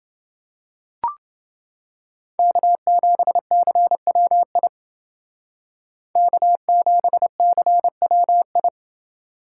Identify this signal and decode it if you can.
Morse code, heard as a keyed tone, about 28 wpm, 700 Hz. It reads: K7CWI K7CWI